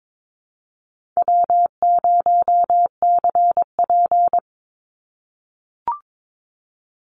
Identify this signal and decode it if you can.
Morse code, heard as a keyed tone, about 22 wpm, 700 Hz. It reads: W0CP
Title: NA SOTA Activator Callsigns 22wpm